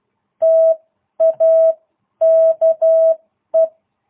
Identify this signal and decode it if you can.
Morse code, heard as a keyed tone, about 12 wpm, spaced as for 10 wpm, 660 Hz. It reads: TAKE